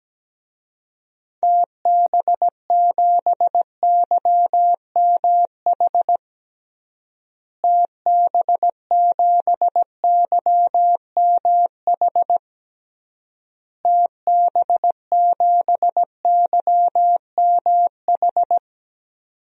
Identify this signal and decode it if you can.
Morse code, heard as a keyed tone, about 17 wpm, 700 Hz. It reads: TB7YMH TB7YMH TB7YMH
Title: DXCC-100 Call Signs - 6 Characters - Repeated 3x - 17wpm